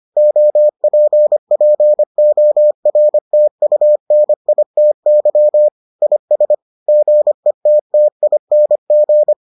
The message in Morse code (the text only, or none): OPPORTUNITY IS GETTING